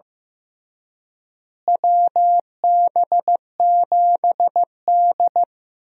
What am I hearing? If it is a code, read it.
Morse code, heard as a keyed tone, about 15 wpm, 700 Hz. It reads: WB7D
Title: USA Call Signs - 4 Characters - Repeated 3x - 15wpm